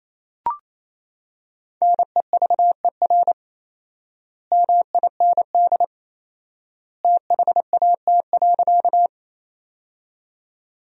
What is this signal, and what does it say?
Morse code, heard as a keyed tone, about 28 wpm, 700 Hz. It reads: NEVER MIND THAT.
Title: 4x Word-Spacing - Sentences from Top 500 Words - 28wpm